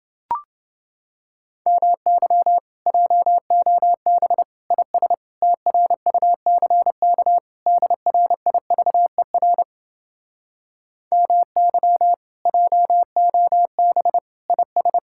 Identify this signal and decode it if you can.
Morse code, heard as a keyed tone, about 31 wpm, 700 Hz. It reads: MY JOB IS TRUCK DRIVER MY JOB IS